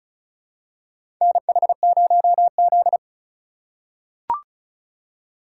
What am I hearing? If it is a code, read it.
Morse code, heard as a keyed tone, about 34 wpm, 700 Hz. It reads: NH0Z
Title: USA Call Signs - 4 Characters - 3x Character Spacing - 35wpm